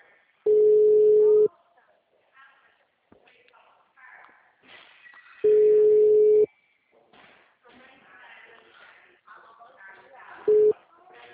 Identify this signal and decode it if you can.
Ringback tone (Europe, 425 Hz), with speech.
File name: out- -2051- - - .13812